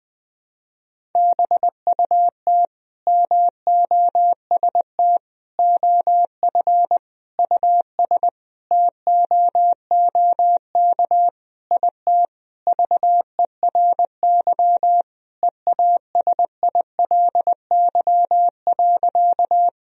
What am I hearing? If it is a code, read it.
Morse code, heard as a keyed tone, about 21 wpm, 700 Hz. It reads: BUT MOST OF US TOOK IT VERY EASILY.
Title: Sentences from Top 1000 Words - Review Part 1 - 20wpm